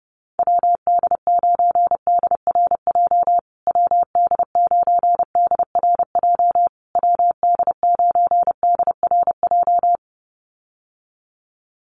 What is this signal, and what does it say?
Morse code, heard as a keyed tone, about 30 wpm, 700 Hz. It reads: WD9DRJ WD9DRJ WD9DRJ
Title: USA Call Signs Repeated 3x 30wpm